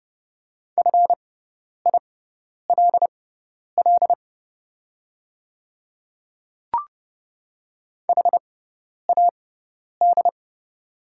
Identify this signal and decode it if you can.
Morse code, heard as a keyed tone, about 30 wpm, spaced as for 10 wpm, 700 Hz. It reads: FILL HAD
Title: 6x Character Spacing - Top 500 Words - 30wpm